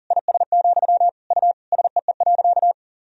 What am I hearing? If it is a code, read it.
Morse code, heard as a keyed tone, about 40 wpm, 700 Hz. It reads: IS, U SEE.
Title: Sentences from Top 100 Words and Abbreviations 40wpm